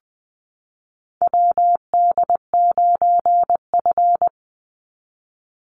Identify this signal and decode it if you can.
Morse code, heard as a keyed tone, about 20 wpm, 700 Hz. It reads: WD9F
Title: NA SOTA Chaser Callsigns 20wpm